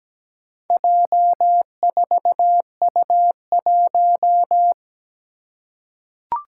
Morse code, heard as a keyed tone, about 17 wpm, 700 Hz. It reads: J4U1
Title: Four Challenging Letter-Number Combinations 17wpm